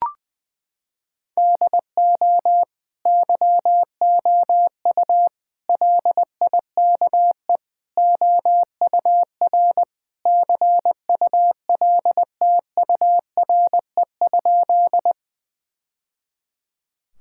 Morse code, heard as a keyed tone, about 20 wpm, 700 Hz. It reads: DO YOU LIKE OUR CULTURE?
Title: Sentences from Top 1000 Words - Review Part 6 - 20wpm